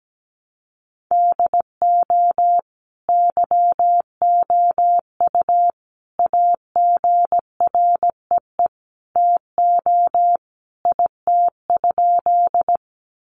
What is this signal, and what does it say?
Morse code, heard as a keyed tone, about 17 wpm, 700 Hz. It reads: DO YOU AGREE TO IT?
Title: Sentences from Top 1100 Words 17wpm